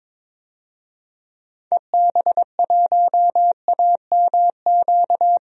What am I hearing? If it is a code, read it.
Morse code, heard as a keyed tone, about 22 wpm, 700 Hz. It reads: EB1AMQ
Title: DXCC-100 Call Signs - 6 Characters - 22wpm